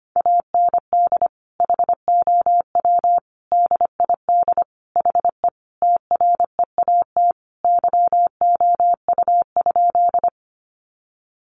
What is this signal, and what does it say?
Morse code, heard as a keyed tone, about 25 wpm, 700 Hz. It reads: AND HOW DID HE TREAT YOU?
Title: Sentences from Top 1300 Words 25wpm